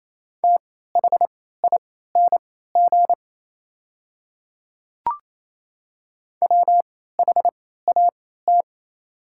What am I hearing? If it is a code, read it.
Morse code, heard as a keyed tone, about 28 wpm, spaced as for 16 wpm, 700 Hz. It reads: THING WHAT
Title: 3x Character Spacing - Top 200 Words - 28wpm